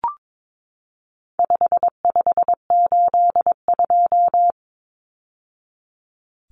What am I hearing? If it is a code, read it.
Morse code, heard as a keyed tone, about 22 wpm, 700 Hz. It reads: 5582
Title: Numbers - 4-Digit 22wpm